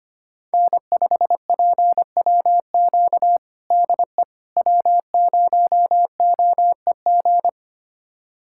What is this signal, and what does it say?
Morse code, heard as a keyed tone, about 25 wpm, 700 Hz. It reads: N5PWQ DE W0OEG
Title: USA Callsign DE Callsign - 5 Characters - 25wpm